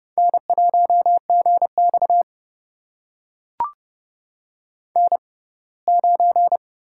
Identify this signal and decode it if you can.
Morse code, heard as a keyed tone, about 30 wpm, spaced as for 20 wpm, 700 Hz. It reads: N1GX N9
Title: USA Call Signs - 6x Character Spacing - 30wpm